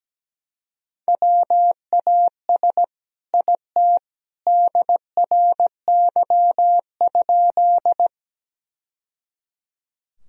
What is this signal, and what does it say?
Morse code, heard as a keyed tone, about 17 wpm, 700 Hz. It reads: WAS IT DRY?
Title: Sentences from Top 1000 Words - Review Part 3 - 17wpm